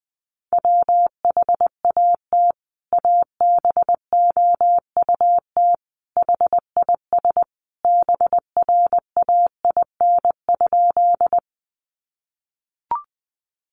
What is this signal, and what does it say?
Morse code, heard as a keyed tone, about 20 wpm, 700 Hz. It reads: WHAT ABOUT HIS BRAIN?
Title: Sentences from Top 1100 Words 20wpm